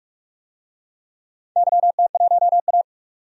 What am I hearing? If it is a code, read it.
Morse code, heard as a keyed tone, about 44 wpm, 700 Hz. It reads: YT1A
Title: DXCC-100 Call Signs - 4 Characters - 45wpm